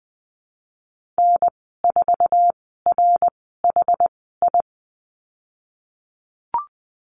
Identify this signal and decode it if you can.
Morse code, heard as a keyed tone, about 20 wpm, spaced as for 15 wpm, 700 Hz. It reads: N4RHI
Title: 069 Lesson 12g - Letter d - Call Signs 20wpm